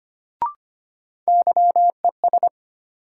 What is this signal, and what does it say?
Morse code, heard as a keyed tone, about 25 wpm, 700 Hz. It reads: YES
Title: Sentences from Top 500 Words - New Word Drill - 25wpm